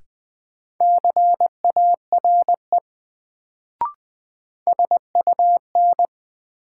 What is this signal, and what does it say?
Morse code, heard as a keyed tone, about 21 wpm, 700 Hz. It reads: CARE SUN
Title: Sentences from Top 300 Words - New Word Drill - 20wpm